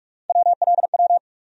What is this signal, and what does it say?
Morse code, heard as a keyed tone, about 44 wpm, 700 Hz. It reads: WLW
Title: USA Call Sign Postfixes - 3 Letters 45wpm